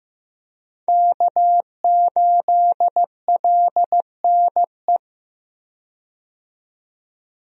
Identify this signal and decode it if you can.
Morse code, heard as a keyed tone, about 15 wpm, 700 Hz. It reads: K8LNE
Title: USA Call Signs - 5 Characters - 15wpm